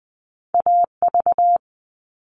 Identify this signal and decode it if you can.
Morse code, heard as a keyed tone, about 20 wpm, 700 Hz. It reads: AV
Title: All Cut Numbers - 2-Digit 20wpm